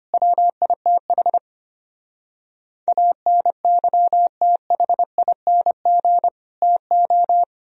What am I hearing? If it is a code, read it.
Morse code, heard as a keyed tone, about 32 wpm, 700 Hz. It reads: WITH ANYTHING TO